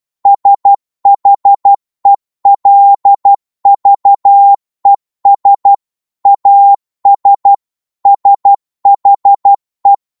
Morse code, heard as a keyed tone, about 12 wpm, 800 Hz. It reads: SHELVES AS SHE
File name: cwmp3 / psg-1-12wpm